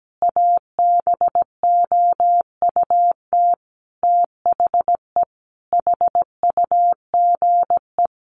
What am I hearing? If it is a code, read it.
Morse code, heard as a keyed tone, about 17 wpm, 700 Hz. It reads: ABOUT THE HUGE
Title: Sentences from Top 1100 Words 17wpm